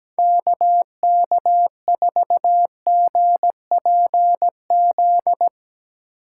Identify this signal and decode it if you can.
Morse code, heard as a keyed tone, about 17 wpm, 700 Hz. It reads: KK4GPZ
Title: USA Call Signs - 6 Characters - 8x Character Spacing - 17wpm